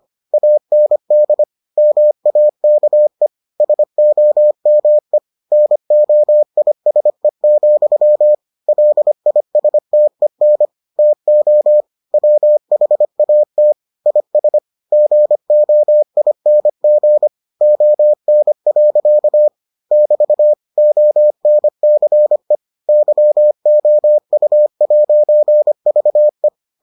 Morse code, heard as a keyed tone, about 25 wpm, 600 Hz. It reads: AND MAKE SOME NOISE, LISTEN TO WHAT IS GOING ON. <BT> ONCE YOU'VE